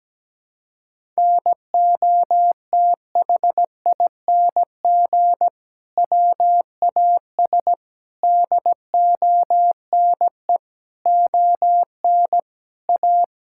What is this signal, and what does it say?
Morse code, heard as a keyed tone, about 17 wpm, 700 Hz. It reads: NOTHING WAS DONE ON A
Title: Sentences from Top 1000 Words - Review Part 5 - 17wpm